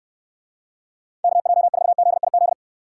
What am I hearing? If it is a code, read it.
Morse code, heard as a keyed tone, about 66 wpm, 700 Hz. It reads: KJ4ZIZ